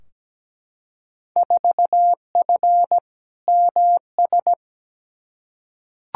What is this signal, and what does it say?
Morse code, heard as a keyed tone, about 17 wpm, 700 Hz. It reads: 4F MS
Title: Field Day Exchanges Encore - 17wpm